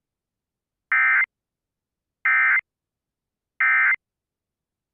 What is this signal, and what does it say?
Emergency Alert System (SAME) tones, heard as harsh buzzing bursts. It is heard from 0.9 to 3.9 s.